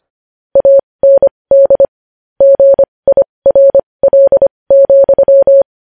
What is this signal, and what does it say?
Morse code, heard as a keyed tone, about 25 wpm, spaced as for 20 wpm, 550 Hz. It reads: AND GIRL,
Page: News Headlines in Morse Code at 20 WPM